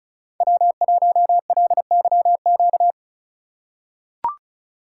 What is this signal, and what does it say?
Morse code, heard as a keyed tone, about 36 wpm, 700 Hz. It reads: W1LYQ
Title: USA Call Signs - 4x Character Spacing - 35wpm